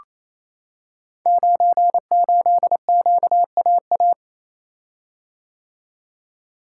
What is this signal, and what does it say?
Morse code, heard as a keyed tone, about 27 wpm, 700 Hz. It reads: 98QAA
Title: Code Groups - 5 Letter:Numbers - 28wpm